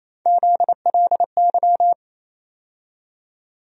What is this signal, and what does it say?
Morse code, heard as a keyed tone, about 28 wpm, 700 Hz. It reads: ZLY
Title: USA Call Sign Postfixes - 3 Letters 28wpm